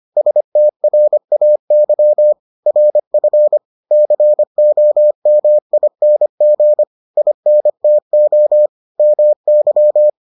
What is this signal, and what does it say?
Morse code, heard as a keyed tone, about 25 wpm, 600 Hz. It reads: STRAY RF COMING INTO MY